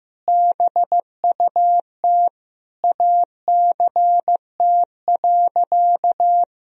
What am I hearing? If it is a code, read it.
Morse code, heard as a keyed tone, about 15 wpm, 700 Hz. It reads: BUT ACT.
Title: Sentences from Top 1000 Words - Review Part 1 - 15wpm